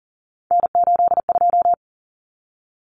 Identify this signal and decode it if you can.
Morse code, heard as a keyed tone, about 40 wpm, 700 Hz. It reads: N82
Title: Common Cut Numbers - 3-Digit 40wpm